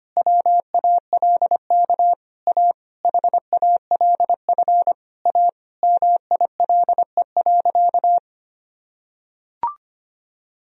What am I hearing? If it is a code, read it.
Morse code, heard as a keyed tone, about 25 wpm, 700 Hz. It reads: WALK A HALF A MILE.
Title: Sentences from Top 1000 Words - Review Part 1 - 25wpm